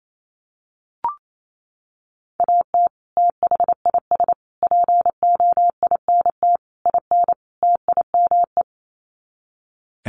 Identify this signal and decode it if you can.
Morse code, heard as a keyed tone, about 28 wpm, 700 Hz. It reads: AT THIS POINT IN TIME
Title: Sets of 5 Words Encore 28wpm